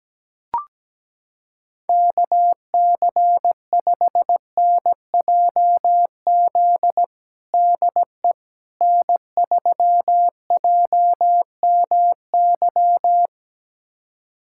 Morse code, heard as a keyed tone, about 17 wpm, 700 Hz. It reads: KC5NJZ DE N3JMY
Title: USA Callsign DE Callsign 17wpm